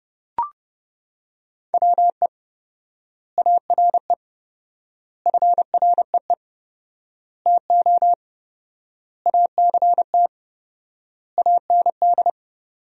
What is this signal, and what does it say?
Morse code, heard as a keyed tone, about 30 wpm, 700 Hz. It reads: WE ARE FREE TO ACT AND
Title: 4x Word-Spacing - Sentences from Top 500 Words - 30wpm